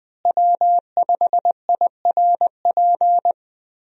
Morse code, heard as a keyed tone, about 20 wpm, 700 Hz. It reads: W5IRP
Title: USA Call Signs - 5 Characters - Repeated 3x - 20wpm